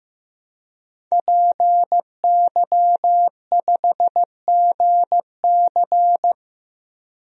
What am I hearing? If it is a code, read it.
Morse code, heard as a keyed tone, about 15 wpm, 700 Hz. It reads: PY5GC
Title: DXCC-100 Call Signs - 5 Characters - Repeated 2x - 15wpm